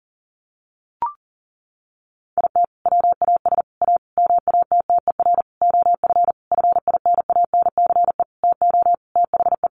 Morse code, heard as a keyed tone, about 40 wpm, 700 Hz. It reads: IT WAS A MATTER OF FINANCE TO THE
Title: Sentences from Top 1500 Words 40wpm